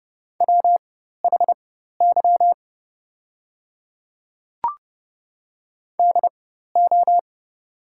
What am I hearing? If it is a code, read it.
Morse code, heard as a keyed tone, about 28 wpm, spaced as for 14 wpm, 700 Hz. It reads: WHY DO
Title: 4x Character Spacing - Top 300 Words - 30wpm